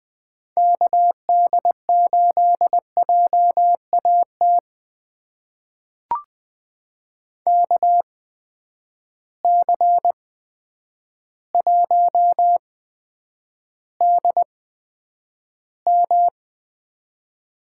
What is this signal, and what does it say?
Morse code, heard as a keyed tone, about 20 wpm, spaced as for 9 wpm, 700 Hz. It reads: KD8JAT KC1DM
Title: USA Call Signs - 6 Characters - 8x Character Spacing - 20wpm